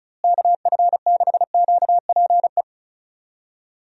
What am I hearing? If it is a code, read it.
Morse code, heard as a keyed tone, about 34 wpm, 700 Hz. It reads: KF6QPE
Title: USA Call Signs - 6 Characters - Repeated 3x - 35wpm